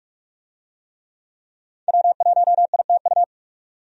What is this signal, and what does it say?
Morse code, heard as a keyed tone, about 43 wpm, 700 Hz. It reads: W1ITU
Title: USA Call Signs - 5 Characters - 45wpm